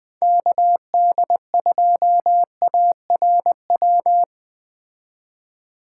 Morse code, heard as a keyed tone, about 20 wpm, 700 Hz. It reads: KD2ARW
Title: Speed Racing - USA 2x3 Call Signs 17wpm